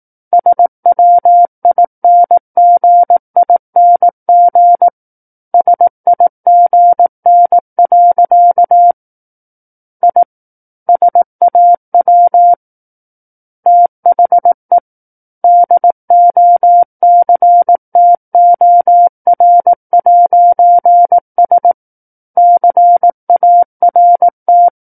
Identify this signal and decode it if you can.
Morse code, heard as a keyed tone, about 18 wpm, 700 Hz. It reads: SWINGING SIGN. I SAW THE DOCTOR'S CART
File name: 18 WPM